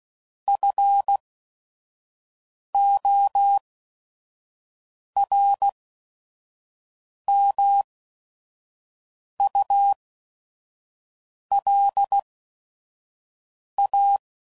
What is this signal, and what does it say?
Morse code, heard as a keyed tone, about 16 wpm, spaced as for 5 wpm, 800 Hz. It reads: FORMULA
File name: CW Code Groups @5WPM / Technical Subjects / XL